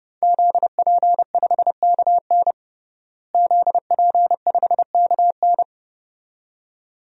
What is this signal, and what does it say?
Morse code, heard as a keyed tone, about 32 wpm, 700 Hz. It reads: ZP5KN ZP5KN
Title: DXCC-100 Call Signs - 5 Characters - Repeated 2x - 30wpm